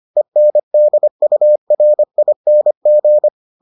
Morse code, heard as a keyed tone, about 25 wpm, 600 Hz. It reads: ENDURING